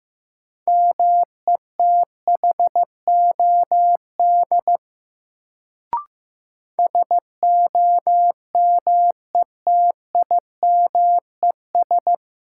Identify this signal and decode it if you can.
Morse code, heard as a keyed tone, about 15 wpm, 700 Hz. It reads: METHOD SOMETIMES
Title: Sentences from Top 600 Words - New Word Drill - 15wpm